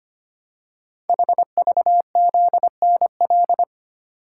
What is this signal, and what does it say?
Morse code, heard as a keyed tone, about 25 wpm, 700 Hz. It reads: HVZNL
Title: Code Groups - 5 Letters - 25wpm